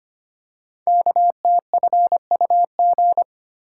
Morse code, heard as a keyed tone, about 25 wpm, 700 Hz. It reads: KTFUG
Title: Code Groups - 5 Letters - 25wpm